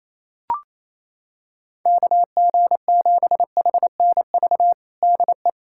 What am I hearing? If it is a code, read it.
Morse code, heard as a keyed tone, about 28 wpm, 700 Hz. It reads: KG7HNV DE
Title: USA Callsign DE Callsign - 6 Characters - 28wpm